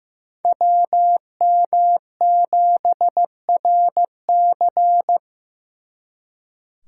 Morse code, heard as a keyed tone, about 18 wpm, 700 Hz. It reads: WM7RC
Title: Speed Racing - USA 2x2 Call Signs 15wpm